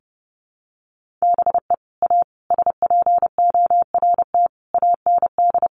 Morse code, heard as a keyed tone, about 30 wpm, 700 Hz. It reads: BE A SPORT AND
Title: Sentences from Top 1500 Words 30wpm